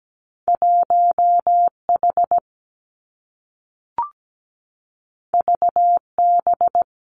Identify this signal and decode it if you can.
Morse code, heard as a keyed tone, about 17 wpm, 700 Hz. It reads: 1 H V B
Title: Two Challenging Letter-Number Combinations 17wpm